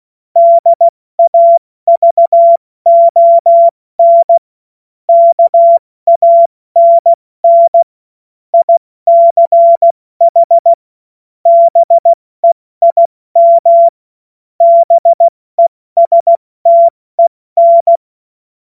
Morse code, heard as a keyed tone, about 16 wpm, 680 Hz. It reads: DAVON KANN ICH BEIM BESTEN
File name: Morsekurs / Witze / Witz 11. April 70 F80